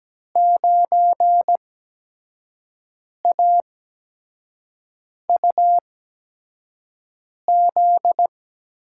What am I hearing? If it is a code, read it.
Morse code, heard as a keyed tone, about 17 wpm, spaced as for 5 wpm, 700 Hz. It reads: 9AUZ
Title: USA Call Signs - 5 Characters - 8x Character Spacing - 17wpm